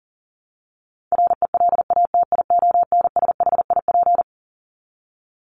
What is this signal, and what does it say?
Morse code, heard as a keyed tone, about 37 wpm, 700 Hz. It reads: RELATIONSHIP
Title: Top 1001-1250 Words 40wpm